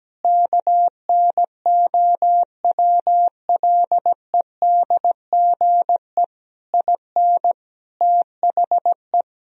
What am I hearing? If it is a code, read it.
Morse code, heard as a keyed tone, about 17 wpm, 700 Hz. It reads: KNOWLEDGE IN THE
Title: Sentences from Top 1000 Words - Review Part 4 - 17wpm